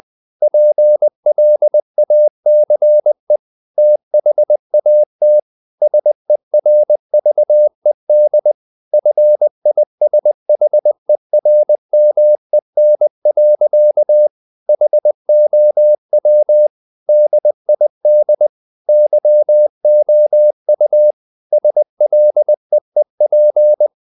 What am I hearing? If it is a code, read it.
Morse code, heard as a keyed tone, about 20 wpm, 600 Hz. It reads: PLACE THAT SERVED FISHERMEN. HOW DID YOU SLEEP
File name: the old man and the sea / the old man and the sea part 1 20wpm